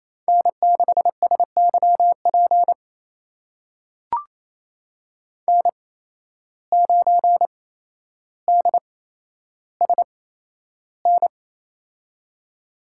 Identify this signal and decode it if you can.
Morse code, heard as a keyed tone, about 27 wpm, spaced as for 12 wpm, 700 Hz. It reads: N6SYP N9DSN
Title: USA Call Signs - 8x Character Spacing - 28wpm